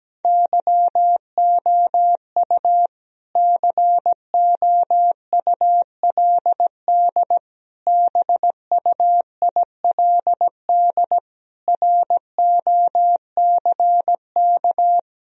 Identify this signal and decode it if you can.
Morse code, heard as a keyed tone, about 17 wpm, 700 Hz. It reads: YOU COULD BUILD ROCK
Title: Sentences from Top 1000 Words - Review Part 2 - 17wpm